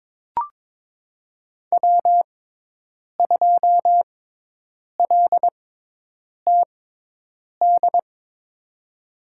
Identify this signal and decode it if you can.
Morse code, heard as a keyed tone, about 22 wpm, spaced as for 8 wpm, 700 Hz. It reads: W2LTD
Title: USA Call Signs - 5 Characters - 6x Character Spacing - 22wpm